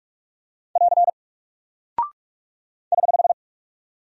Morse code, heard as a keyed tone, about 48 wpm, 700 Hz. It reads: <AR> <HH>
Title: Single Character (Letter:Number:Punctuation:Prosign) Alt 45wpm